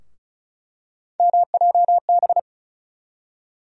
Morse code, heard as a keyed tone, about 34 wpm, 700 Hz. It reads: MJB
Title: USA Call Sign Postfixes - 3 Letters 35wpm